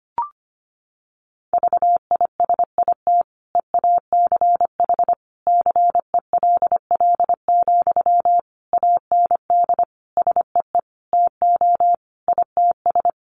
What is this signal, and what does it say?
Morse code, heard as a keyed tone, about 25 wpm, 700 Hz. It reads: VISIT EACH CELL, AND SEE TO ITS
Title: Sentences from Top 1100 Words 25wpm